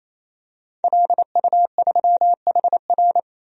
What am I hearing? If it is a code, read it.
Morse code, heard as a keyed tone, about 28 wpm, 700 Hz. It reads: LU3HR
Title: DXCC-100 Call Signs - 5 Characters - 28wpm